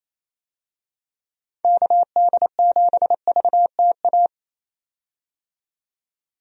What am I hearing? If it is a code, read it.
Morse code, heard as a keyed tone, about 28 wpm, 700 Hz. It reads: KD7VTA
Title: USA Call Signs - 6 Characters - 28wpm